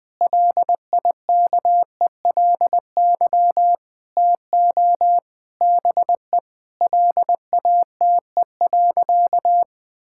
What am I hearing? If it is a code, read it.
Morse code, heard as a keyed tone, about 20 wpm, 700 Hz. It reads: LIKELY TO BE LATE.
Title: Sentences with 4-Word Phrases - Spoken Beginning - 20wpm